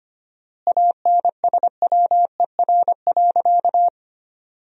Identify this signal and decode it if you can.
Morse code, heard as a keyed tone, about 25 wpm, 700 Hz. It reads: ANSWER.
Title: 4x Word-Spacing - Sentences from Top 300 Words - 25wpm